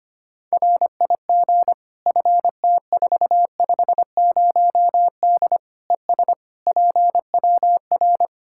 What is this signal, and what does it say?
Morse code, heard as a keyed tone, about 25 wpm, 700 Hz. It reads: RIG FT450D ES PWR
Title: Practice QSO - Exchange 2 - Info - 25wpm